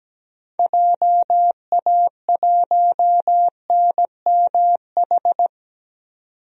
Morse code, heard as a keyed tone, about 17 wpm, 700 Hz. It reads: JA1NMH
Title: DXCC-100 Call Signs - 6 Characters - 17wpm